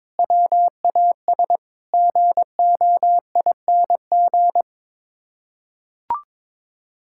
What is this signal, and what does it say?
Morse code, heard as a keyed tone, about 22 wpm, 700 Hz. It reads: WAS GOING
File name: practice / 22wpm / Sets of 7 Words - 22wpm